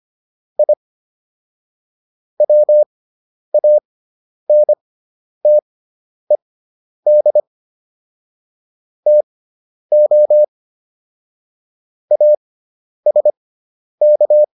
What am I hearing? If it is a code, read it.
Morse code, heard as a keyed tone, about 25 wpm, spaced as for 10 wpm, 600 Hz. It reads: I WANTED TO ASK